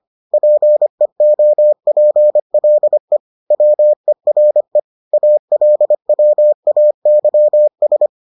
Morse code, heard as a keyed tone, about 25 wpm, 600 Hz. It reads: PEOPLE WERE ALWAYS